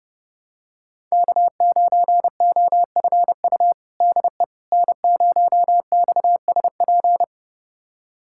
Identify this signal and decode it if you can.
Morse code, heard as a keyed tone, about 32 wpm, 700 Hz. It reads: K9OFU DE N0XSP
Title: USA Callsign DE Callsign - 5 Characters - 30wpm